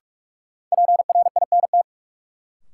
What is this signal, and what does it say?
Morse code, heard as a keyed tone, about 44 wpm, 700 Hz. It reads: PRINT
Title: Sentences from Top 1400 Words - New Word Drill Alt - 45wpm